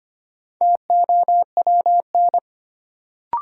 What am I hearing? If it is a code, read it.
Morse code, heard as a keyed tone, about 25 wpm, 700 Hz. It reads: TOWN
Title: Sentences from Top 500 Words - New Word Drill - 25wpm